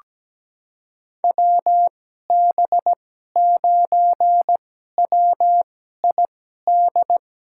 Morse code, heard as a keyed tone, about 17 wpm, spaced as for 12 wpm, 700 Hz. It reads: WB9WID
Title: USA Call Signs - 6 Characters - 2x Character Spacing - 17wpm